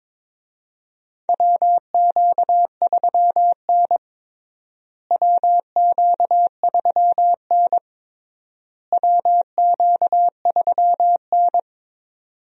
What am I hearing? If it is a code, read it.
Morse code, heard as a keyed tone, about 22 wpm, 700 Hz. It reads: WQ3N WQ3N WQ3N
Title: USA Call Signs - 4 Characters - Repeated 3x - 22wpm